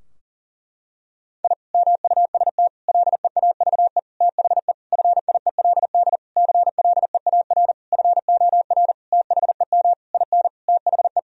Morse code, heard as a keyed tone, about 40 wpm, 700 Hz. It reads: I MUST LEAVE THE FIELD CLEAR FOR THEM IN THE
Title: Sentences from Top 1000 Words - Review Part 2 - 40wpm